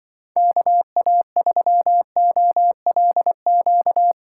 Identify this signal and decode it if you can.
Morse code, heard as a keyed tone, about 23 wpm, 700 Hz. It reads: KA3OLQ
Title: Speed Racing - USA 2x3 Call Signs 20wpm